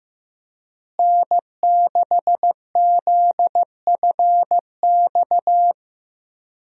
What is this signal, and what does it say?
Morse code, heard as a keyed tone, about 15 wpm, 700 Hz. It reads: N6ZFX
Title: USA Call Signs - 5 Characters - Repeated 3x - 15wpm